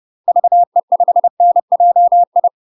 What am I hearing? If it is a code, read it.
Morse code, heard as a keyed tone, about 30 wpm, 700 Hz. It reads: VE5NJI
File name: UK MASTER QSOs TRAINING / 30WPM / UK MASTER QSOs 0051